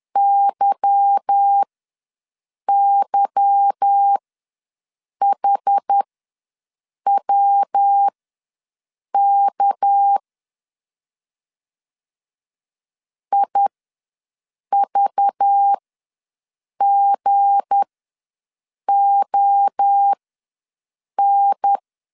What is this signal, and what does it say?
Morse code, heard as a keyed tone, about 10 wpm, spaced as for 6 wpm, 800 Hz. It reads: YYHWK IVGON
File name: LEZIONI / 8 YJ45 / 8 Y ws 10wpm 3952